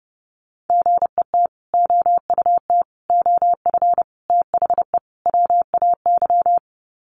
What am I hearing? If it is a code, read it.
Morse code, heard as a keyed tone, about 30 wpm, 700 Hz. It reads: GET OUT OF THE WAY
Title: Sets of 5 Words Encore 30wpm